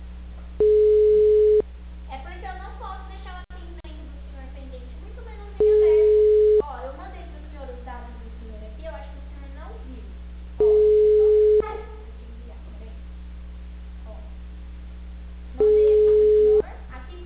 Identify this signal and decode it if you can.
Ringback tone (Europe, 425 Hz), with speech.
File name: out- -2006- - - .38357